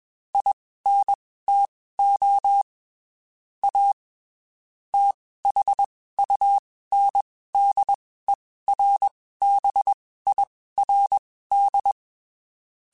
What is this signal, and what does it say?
Morse code, heard as a keyed tone, about 21 wpm, spaced as for 15 wpm, 800 Hz. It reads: INTO A THUNDERBIRD
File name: (spaced) / QSOs Spaced-mp3s / WD0BGP 1 @20 sp